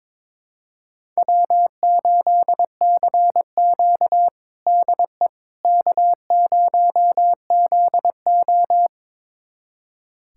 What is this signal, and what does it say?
Morse code, heard as a keyed tone, about 22 wpm, 700 Hz. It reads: W8CQ DE K0ZO
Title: USA Callsign DE Callsign - 4 Characters - 22wpm